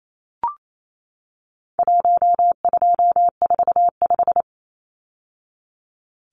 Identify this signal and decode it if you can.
Morse code, heard as a keyed tone, about 28 wpm, 700 Hz. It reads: 1245
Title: Numbers - Contest Style - 4-Digit 28wpm